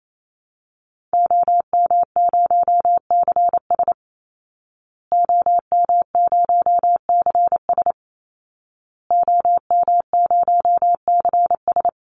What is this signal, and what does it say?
Morse code, heard as a keyed tone, about 28 wpm, 700 Hz. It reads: OM0CS OM0CS OM0CS
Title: Call Signs Repeated 2x - 28wpm